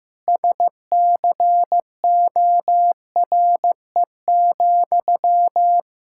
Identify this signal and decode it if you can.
Morse code, heard as a keyed tone, about 15 wpm, 700 Hz. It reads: SCORE,
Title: Sentences from Top 1600 Words 15wpm